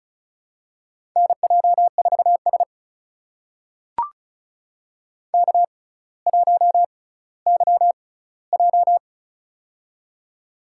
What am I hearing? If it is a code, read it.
Morse code, heard as a keyed tone, about 35 wpm, spaced as for 18 wpm, 700 Hz. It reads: NJ4S K1YJ
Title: USA Call Signs - 4 Characters - 6x Character Spacing - 35wpm